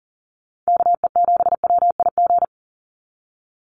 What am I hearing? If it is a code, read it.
Morse code, heard as a keyed tone, about 40 wpm, 700 Hz. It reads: KE7WIG
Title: USA Call Signs 40wpm